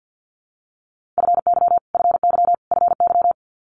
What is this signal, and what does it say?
Morse code, heard as a keyed tone, about 50 wpm, 690 Hz. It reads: FY FY FY
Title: Multi-letter Phonemes Repeated 3x 50wpm